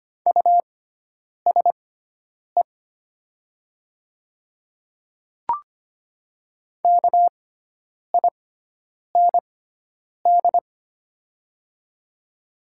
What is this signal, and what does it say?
Morse code, heard as a keyed tone, about 25 wpm, spaced as for 9 wpm, 700 Hz. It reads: USE KIND
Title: 6x Character Spacing - Top 500 Words - 25wpm